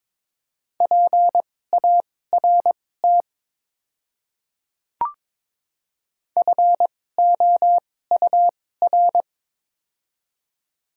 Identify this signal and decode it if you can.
Morse code, heard as a keyed tone, about 22 wpm, spaced as for 16 wpm, 700 Hz. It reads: PART FOUR
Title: 2x Character Spacing - Top 500 Words - 22wpm